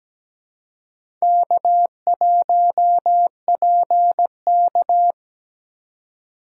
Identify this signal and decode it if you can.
Morse code, heard as a keyed tone, about 17 wpm, 700 Hz. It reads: K1PK
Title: USA Call Signs - 4 Characters - 3x Character Spacing - 17wpm